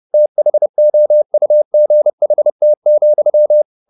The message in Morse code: THOUGHT,

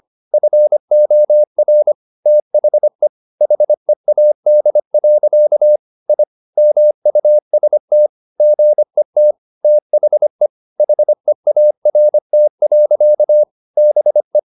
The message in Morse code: FOR THE HEAD. I MUST GET THE HEART. BE

